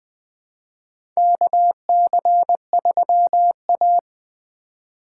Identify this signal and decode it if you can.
Morse code, heard as a keyed tone, about 20 wpm, 700 Hz. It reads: KC3A
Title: USA Call Signs - 4 Characters - 2x Character Spacing - 20wpm